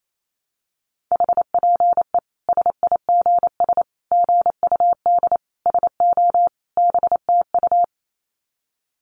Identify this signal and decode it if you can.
Morse code, heard as a keyed tone, about 28 wpm, 700 Hz. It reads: HPE SIGS GUD SO BTU
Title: Common QSO Phrases 28wpm